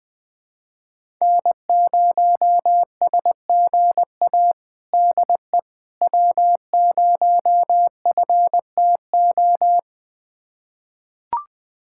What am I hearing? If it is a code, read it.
Morse code, heard as a keyed tone, about 21 wpm, 700 Hz. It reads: N0SGA DE W0FTO
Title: USA Callsign DE Callsign - 5 Characters - 20wpm